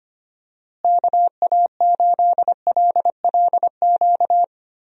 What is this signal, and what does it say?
Morse code, heard as a keyed tone, about 25 wpm, 700 Hz. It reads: KA8LLQ
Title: USA Call Signs - 6 Characters - Repeated 3x - 25wpm